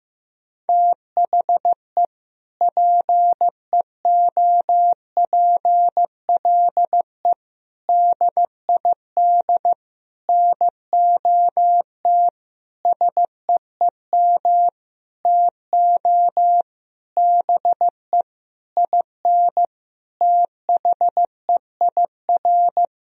Morse code, heard as a keyed tone, about 15 wpm, 700 Hz. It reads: THE PEOPLE DID NOT SEEM TO BE IN THEIR
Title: Sentences from Top 1000 Words - Review Part 1 - 15wpm